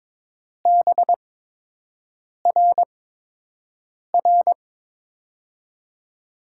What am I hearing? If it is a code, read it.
Morse code, heard as a keyed tone, about 22 wpm, spaced as for 6 wpm, 700 Hz. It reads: BRR
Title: USA Call Signs - 8x Character Spacing - 22wpm